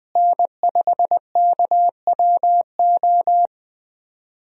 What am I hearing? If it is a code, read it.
Morse code, heard as a keyed tone, about 20 wpm, 700 Hz. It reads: N5KWO
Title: USA Call Signs - 5 Characters - Repeated 3x - 20wpm